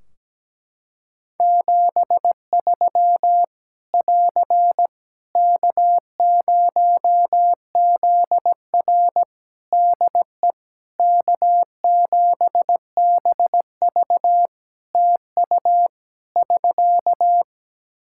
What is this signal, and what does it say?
Morse code, heard as a keyed tone, about 17 wpm, 700 Hz. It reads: 73 <AR> K0ZR DE K7BV TU <SK>
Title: K1USN SST QSO Practice 17wpm